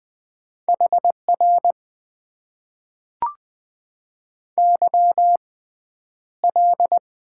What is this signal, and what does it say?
Morse code, heard as a keyed tone, about 20 wpm, spaced as for 10 wpm, 700 Hz. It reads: HR YL
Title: Top 90 QSO Elements - 6x Character Spacing - 20wpm